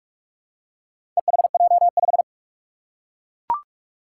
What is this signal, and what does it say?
Morse code, heard as a keyed tone, about 43 wpm, 700 Hz. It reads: EHJ5
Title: Four Challenging Letter-Number Combinations 45wpm